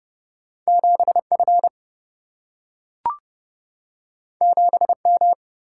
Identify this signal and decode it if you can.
Morse code, heard as a keyed tone, about 30 wpm, 700 Hz. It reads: 7 F 7 M
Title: Single-Letter Single-Number 30wpm